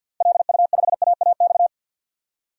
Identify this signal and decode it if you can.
Morse code, heard as a keyed tone, about 46 wpm, 700 Hz. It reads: LU5AAX
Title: DXCC-100 Call Signs - 6 Characters - Repeated 3x - 50wpm